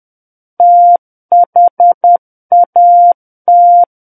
Morse code, heard as a keyed tone, about 10 wpm, 700 Hz. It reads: THAT
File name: Book One / 01-Chapter 1 - 10 WPM